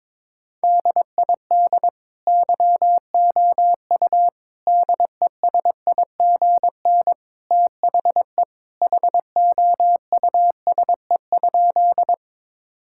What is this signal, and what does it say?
Morse code, heard as a keyed tone, about 21 wpm, 700 Hz. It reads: DID YOU DESIGN THE HOUSE?
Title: Sentences from Top 1000 Words - Review Part 5 - 22wpm